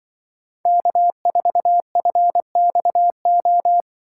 Morse code, heard as a keyed tone, about 23 wpm, 700 Hz. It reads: K4FXO
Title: Speed Racing - USA Call Signs 20wpm